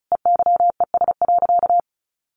Morse code, heard as a keyed tone, about 34 wpm, 700 Hz. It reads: EYES.
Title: Easy Phrases 35wpm